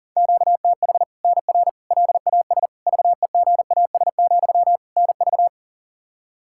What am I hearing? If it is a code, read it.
Morse code, heard as a keyed tone, about 40 wpm, 700 Hz. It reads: QTH NR LAS VEGAS, NV
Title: Practice QSO - Exchange 1 - Beginning - 40wpm